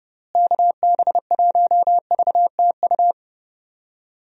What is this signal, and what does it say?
Morse code, heard as a keyed tone, about 30 wpm, 700 Hz. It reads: KB1VTU
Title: USA Call Signs - 6 Characters - Repeated 3x - 30wpm